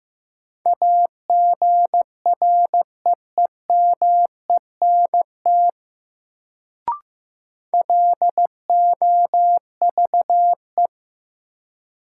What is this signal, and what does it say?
Morse code, heard as a keyed tone, about 15 wpm, 700 Hz. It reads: AGREEMENT LOVE
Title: Sets of 5 Words Encore - New Word Drill - 15wpm